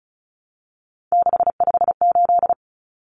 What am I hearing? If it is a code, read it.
Morse code, heard as a keyed tone, about 34 wpm, 700 Hz. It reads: 658
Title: Numbers - 3-Digit 35wpm